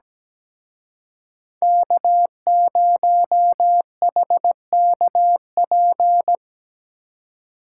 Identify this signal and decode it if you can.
Morse code, heard as a keyed tone, about 20 wpm, 700 Hz. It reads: K0HKP
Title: Speed Racing - USA Call Signs 17wpm